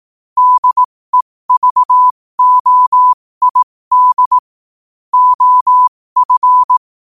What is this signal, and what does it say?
Morse code, heard as a keyed tone, about 18 wpm, spaced as for 15 wpm, 1000 Hz. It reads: DEVOID OF
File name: qotdfiles / 15wpm